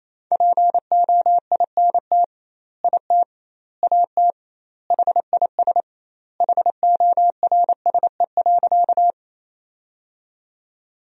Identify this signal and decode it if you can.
Morse code, heard as a keyed tone, about 28 wpm, 700 Hz. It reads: POINT IT AT HIS HORSE.
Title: 2x Word-Spacing - Sentences from Top 300 Words - 28wpm